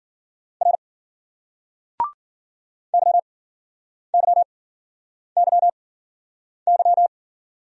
Morse code, heard as a keyed tone, about 52 wpm, spaced as for 9 wpm, 700 Hz. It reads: R YYYY